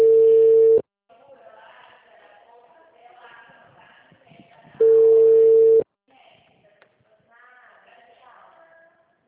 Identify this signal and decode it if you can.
Ringback tone (China, 450 Hz), with speech.
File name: out- -2018- - - .9896